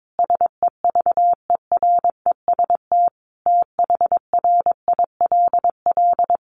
Morse code, heard as a keyed tone, about 22 wpm, 700 Hz. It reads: SEVEREST THRILL
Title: A Sound Of Thunder 22wpm